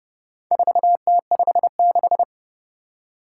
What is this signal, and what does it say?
Morse code, heard as a keyed tone, about 31 wpm, 700 Hz. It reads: 4T56
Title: Four Challenging Letter-Number Combinations 30wpm